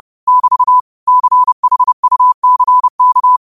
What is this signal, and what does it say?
Morse code, heard as a keyed tone, about 30 wpm, 1000 Hz. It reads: <BT> CHUCK
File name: 30wpm